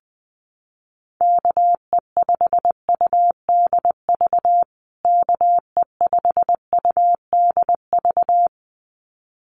Morse code, heard as a keyed tone, about 20 wpm, 700 Hz. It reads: KE5UDV KE5UDV
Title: USA Call Signs Repeated 2x 20wpm